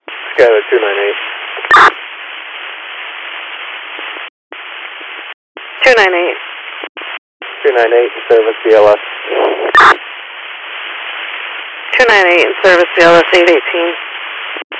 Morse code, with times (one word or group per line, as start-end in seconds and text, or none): none